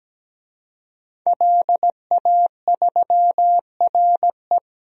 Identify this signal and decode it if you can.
Morse code, heard as a keyed tone, about 17 wpm, 700 Hz. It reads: LA3RE
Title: DXCC-100 Call Signs - 5 Characters - Repeated 2x - 17wpm